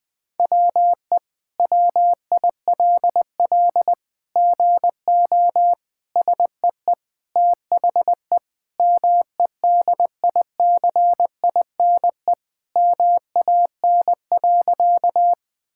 Morse code, heard as a keyed tone, about 20 wpm, 700 Hz. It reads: WE WILL GO SEE THE MEDICINE MAN.
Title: Sentences from Top 1000 Words - Review Part 6 - 20wpm